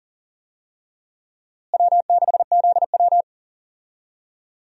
Morse code, heard as a keyed tone, about 37 wpm, 700 Hz. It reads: W6ZW
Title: USA Call Signs - 4 Characters - Repeated 2x - 40wpm